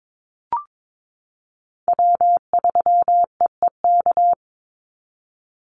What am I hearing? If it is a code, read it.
Morse code, heard as a keyed tone, about 22 wpm, 700 Hz. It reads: W3EEK
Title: Call Signs Repeated 3x - 22wpm